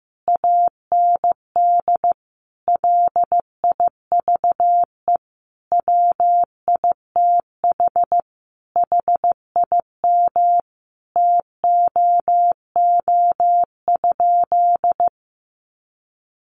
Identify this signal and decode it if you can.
Morse code, heard as a keyed tone, about 14 wpm, 700 Hz. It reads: AND LIVE WITH HIM TOO?
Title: Sentences from Top 500 Words 15wpm